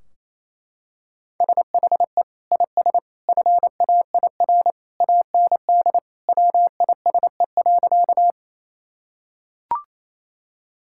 Morse code, heard as a keyed tone, about 28 wpm, 700 Hz. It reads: SHE IS FAIR AND WISE.
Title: Sentences from Top 1600 Words 28wpm